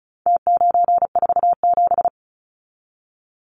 Morse code, heard as a keyed tone, about 34 wpm, 700 Hz. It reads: T947
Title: Numbers - Contest Style - 4-Digit 35wpm